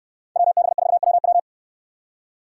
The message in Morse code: WB4WW

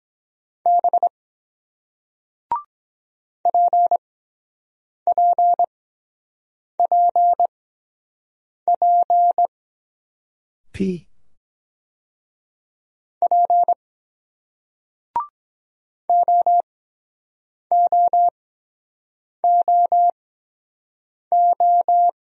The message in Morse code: B PPPP P OOOO

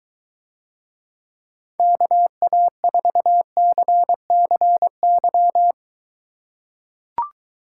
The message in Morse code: KA4CCY

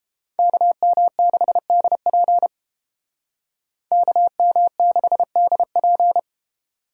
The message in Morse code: KM6DP KM6DP

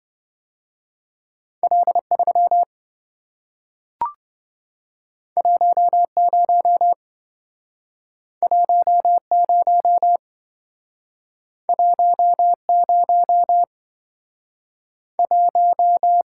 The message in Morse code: L3 10 10 10 1

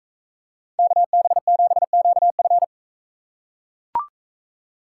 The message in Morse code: KB7QF

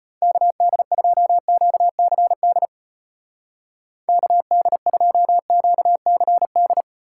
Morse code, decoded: KD2QCD KD2QCD

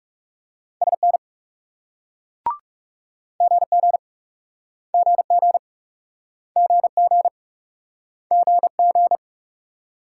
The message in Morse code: SN GG GG GG GG

